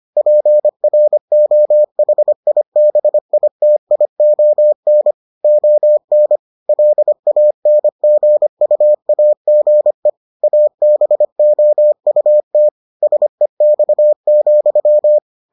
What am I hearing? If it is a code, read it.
Morse code, heard as a keyed tone, about 25 wpm, 600 Hz. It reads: PROHIBITION ON LANGUAGE ABOUT SEX,